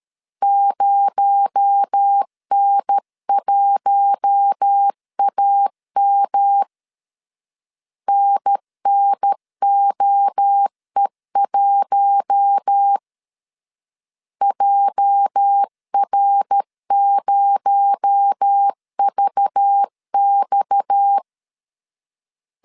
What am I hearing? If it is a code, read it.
Morse code, heard as a keyed tone, about 13 wpm, 800 Hz. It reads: 0N1AM NNOE1 JR0VX
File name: CW / DET / 1 9 1500C S C 12wpm 3811